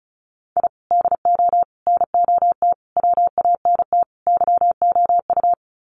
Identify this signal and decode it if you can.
Morse code, heard as a keyed tone, about 34 wpm, 700 Hz. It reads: I DO NOT WANT YOU
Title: Sets of 5 Words 35wpm